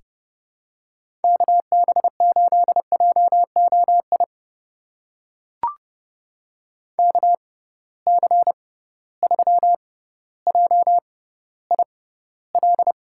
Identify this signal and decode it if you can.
Morse code, heard as a keyed tone, about 30 wpm, spaced as for 15 wpm, 700 Hz. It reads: KB8JOI KC3JIL
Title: USA Call Signs - 6 Characters - 6x Character Spacing - 30wpm